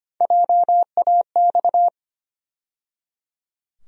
Morse code, heard as a keyed tone, about 25 wpm, 700 Hz. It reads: JAX
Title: USA Call Sign Postfixes - 3 Letters 25wpm